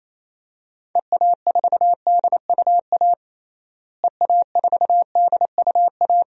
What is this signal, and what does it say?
Morse code, heard as a keyed tone, about 28 wpm, 700 Hz. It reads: EA4DUA EA4DUA
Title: DXCC-100 Call Signs - 6 Characters - Repeated 3x - 28wpm